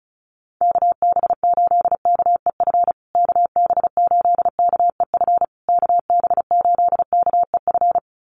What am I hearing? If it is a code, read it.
Morse code, heard as a keyed tone, about 34 wpm, 700 Hz. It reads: KB8KEF KB8KEF KB8KEF
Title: USA Call Signs Repeated 3x 35wpm